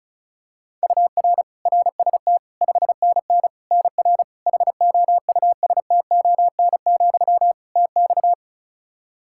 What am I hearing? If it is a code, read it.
Morse code, heard as a keyed tone, about 34 wpm, 700 Hz. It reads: UR RST 5NN NR HOUSTON, TX
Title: K1USN SST QSO Practice 35wpm